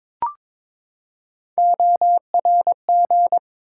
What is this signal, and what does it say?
Morse code, heard as a keyed tone, about 22 wpm, 700 Hz. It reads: ORG
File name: ARRL Sections v2 - 22wpm